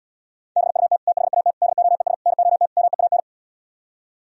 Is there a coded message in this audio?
Yes: Morse code from 0.6 to 3.2 s, heard as a keyed tone, about 77 wpm, 700 Hz. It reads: BUT THAT DOES NOT MEAN